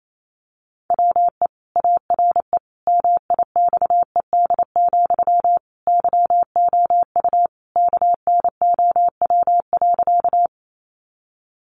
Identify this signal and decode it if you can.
Morse code, heard as a keyed tone, about 28 wpm, 700 Hz. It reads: WE ARE MIXED, YOU KNOW.
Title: Sentences from Top 1500 Words 28wpm